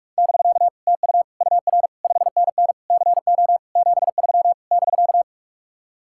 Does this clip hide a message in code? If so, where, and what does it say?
Yes: Morse code from 0.2 to 5.2 s, heard as a keyed tone, about 49 wpm, 700 Hz. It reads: <BK> TU UR 5NN CO 73 <BK>